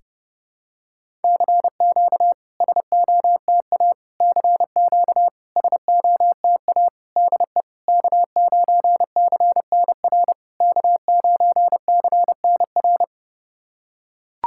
Morse code, heard as a keyed tone, about 32 wpm, 690 Hz. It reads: CQ SOTA CQ SOTA DE K9CNR K9CNR
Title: SOTA QSO Practice 30wpm